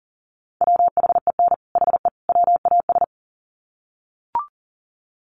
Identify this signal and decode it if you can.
Morse code, heard as a keyed tone, about 37 wpm, 700 Hz. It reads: WHEN HE WAS
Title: Sets of 3 Words 40wpm